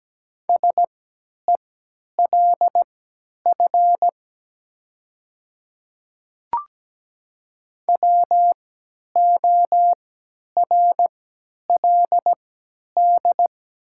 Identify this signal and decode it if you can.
Morse code, heard as a keyed tone, about 17 wpm, spaced as for 10 wpm, 700 Hz. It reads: SELF WORLD
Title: 3x Character Spacing - Top 200 Words - 17wpm